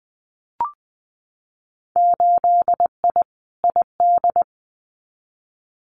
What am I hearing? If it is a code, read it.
Morse code, heard as a keyed tone, about 19 wpm, 700 Hz. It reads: 8I ID
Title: Winter Field Day Exchanges Encore 20wpm